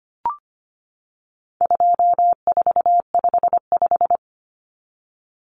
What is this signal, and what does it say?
Morse code, heard as a keyed tone, about 25 wpm, 700 Hz. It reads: 2455
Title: Common Cut Numbers - 4-Digit 25wpm